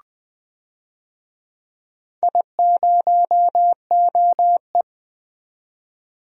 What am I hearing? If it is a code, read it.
Morse code, heard as a keyed tone, about 21 wpm, 700 Hz. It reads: I0OE
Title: DXCC-100 Call Signs - 4 Characters - 20wpm